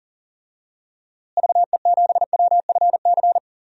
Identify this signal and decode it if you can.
Morse code, heard as a keyed tone, about 40 wpm, 700 Hz. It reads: VE7WFC
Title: DXCC-100 Call Signs - 6 Characters - Repeated 3x - 40wpm